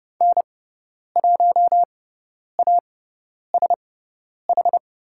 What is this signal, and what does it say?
Morse code, heard as a keyed tone, about 30 wpm, spaced as for 10 wpm, 700 Hz. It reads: N1ASH
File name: Lesson 17g - Letter u - Call Signs 30wpm